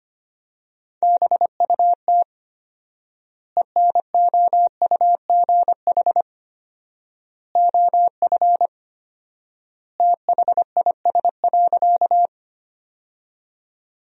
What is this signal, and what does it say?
Morse code, heard as a keyed tone, about 25 wpm, 700 Hz. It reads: BUT ENOUGH OF THIS.
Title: 4x Word-Spacing - Sentences from Top 400 Words - 25wpm